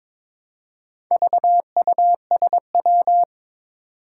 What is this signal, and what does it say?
Morse code, heard as a keyed tone, about 22 wpm, 700 Hz. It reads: VUSW
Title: Four Challenging Letter-Number Combinations 22wpm